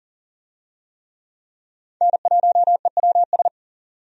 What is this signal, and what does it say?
Morse code, heard as a keyed tone, about 40 wpm, 700 Hz. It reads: N1EWS
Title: USA Call Signs - 5 Characters - 40wpm